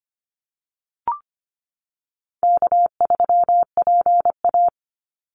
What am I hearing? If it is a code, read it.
Morse code, heard as a keyed tone, about 25 wpm, 700 Hz. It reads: K3PA
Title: Call Signs Repeated 2x - 25wpm